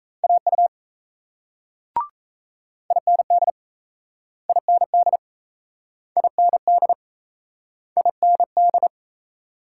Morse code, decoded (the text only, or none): AU IND IND IND IND